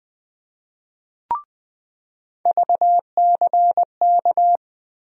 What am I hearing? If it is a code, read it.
Morse code, heard as a keyed tone, about 20 wpm, 700 Hz. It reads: VCK
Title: USA Call Sign Postfixes - 3 Letters 20wpm